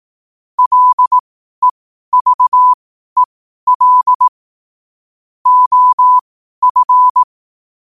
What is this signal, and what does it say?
Morse code, heard as a keyed tone, about 18 wpm, spaced as for 12 wpm, 1000 Hz. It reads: LEVEL OF
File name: qotdfiles / 12wpm